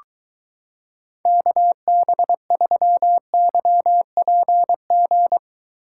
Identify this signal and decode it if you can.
Morse code, heard as a keyed tone, about 22 wpm, 700 Hz. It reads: KB3YPG